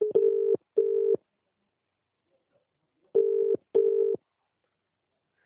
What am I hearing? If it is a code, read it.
Telephone call progress tone (UK), heard as a telephone network's tone, about 400, 450 Hz. Ringback tone.